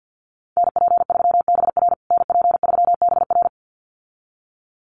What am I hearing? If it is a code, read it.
Morse code, heard as a keyed tone, about 50 wpm, 700 Hz. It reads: NP3BR NP3BR
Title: USA Call Signs Repeated 2x 50wpm